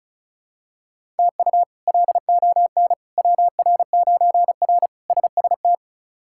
Morse code, heard as a keyed tone, about 34 wpm, 700 Hz. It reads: TU LON WR9R SST
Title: K1USN SST QSO Practice - 35wpm